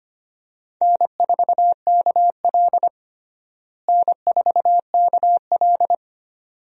Morse code, heard as a keyed tone, about 25 wpm, 700 Hz. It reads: N4KL N4KL
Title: USA Call Signs - 4 Characters - Repeated 2x - 25wpm